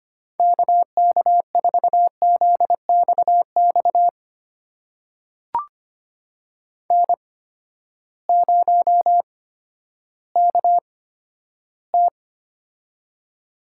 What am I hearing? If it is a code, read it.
Morse code, heard as a keyed tone, about 25 wpm, spaced as for 12 wpm, 700 Hz. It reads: KK4ZXX N0KT
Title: USA Call Signs - 8x Character Spacing - 25wpm